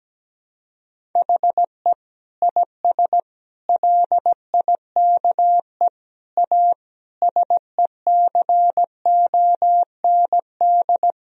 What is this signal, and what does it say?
Morse code, heard as a keyed tone, about 17 wpm, 700 Hz. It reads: HE IS LIKE A SECOND
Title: Sentences from Top 1000 Words - Review Part 1 - 17wpm